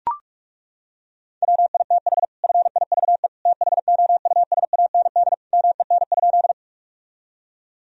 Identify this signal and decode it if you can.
Morse code, heard as a keyed tone, about 41 wpm, 700 Hz. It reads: WITH FIVE THOUSAND MEN?
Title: Sentences from Top 1000 Words - Review Part 3 - 45wpm